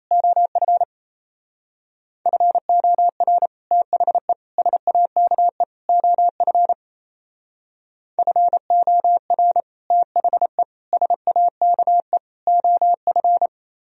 Morse code, heard as a keyed tone, about 35 wpm, 700 Hz. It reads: OF FOR THE SAKE OF FOR THE SAKE OF
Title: Speed-Racing - Sets of 4 Words 28wpm